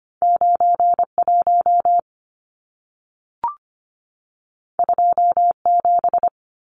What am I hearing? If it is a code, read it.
Morse code, heard as a keyed tone, about 25 wpm, 700 Hz. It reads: 91 27
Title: Numbers - 2-Digit 25wpm